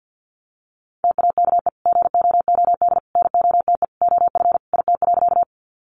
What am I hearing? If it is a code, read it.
Morse code, heard as a keyed tone, about 50 wpm, 700 Hz. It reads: TAKE GOOD NOTE OF IT.
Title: Sentences from Top 500 Words 50wpm